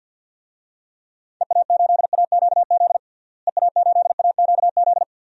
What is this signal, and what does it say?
Morse code, heard as a keyed tone, about 51 wpm, 700 Hz. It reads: EA8AQZ EA8AQZ
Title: DXCC-100 Call Signs - 6 Characters - Repeated 2x - 50wpm